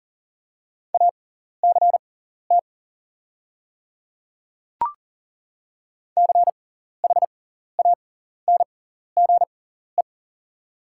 Morse code, heard as a keyed tone, about 39 wpm, spaced as for 14 wpm, 700 Hz. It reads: ACT CHANGE
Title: 6x Character Spacing - Top 200 Words - 40wpm